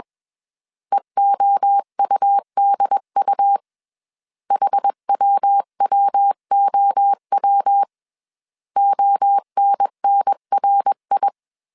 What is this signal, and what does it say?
Morse code, heard as a keyed tone, about 21 wpm, 800 Hz. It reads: EOUDU HWWOW ONNRI
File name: CW / DET / 1 3 1500C S B 20wpm 1754